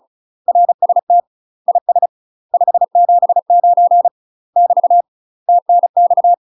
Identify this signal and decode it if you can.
Morse code, heard as a keyed tone, about 36 wpm, 700 Hz. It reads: RST IS 579 <BT> TNX
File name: UK MASTER QSOs TRAINING / 35WPM / UK MASTER QSOs 0009